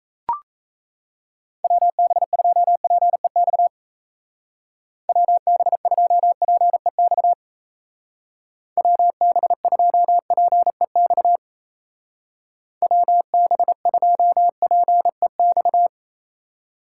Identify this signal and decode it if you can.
Morse code, heard as a keyed tone, about 34 wpm, 700 Hz. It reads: WB2PEX WB2PEX WB2PEX WB2PEX
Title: Speed Racing - USA 2x3 Call Signs 28wpm